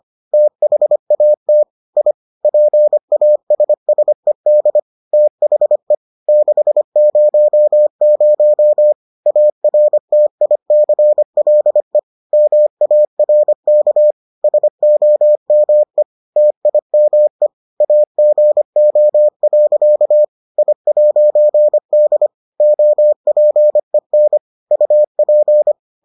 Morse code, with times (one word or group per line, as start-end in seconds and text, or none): 0.3-1.6 THAT
2.0-2.1 I
2.4-4.8 PASSED
5.1-5.9 THE
6.3-8.9 600
9.3-12.0 ARTICLE
12.3-14.1 MARK
14.4-16.0 SOME
16.4-17.5 TIME
17.8-20.2 AGO.
20.6-22.3 I'D
22.6-24.4 OPEN
24.7-25.7 UP